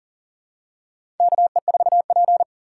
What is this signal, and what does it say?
Morse code, heard as a keyed tone, about 40 wpm, 700 Hz. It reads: KE4P
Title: USA Call Signs - 4 Characters - 4x Character Spacing - 40wpm